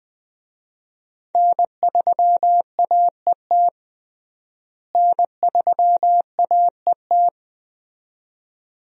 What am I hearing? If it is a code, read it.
Morse code, heard as a keyed tone, about 20 wpm, 700 Hz. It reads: N3AET N3AET
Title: USA Call Signs - 5 Characters - Repeated 2x - 20wpm